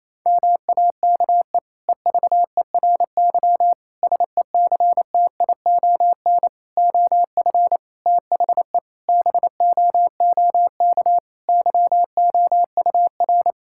Morse code, heard as a keyed tone, about 28 wpm, 700 Hz. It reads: MAKE EVERY SECTION OF THE BOOK YOUR
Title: Sentences from Top 1000 Words - Review Part 5 - 28wpm